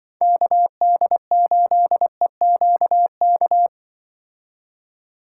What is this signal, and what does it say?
Morse code, heard as a keyed tone, about 24 wpm, 700 Hz. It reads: KD8EQK
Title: Speed Racing - USA 2x3 Call Signs 20wpm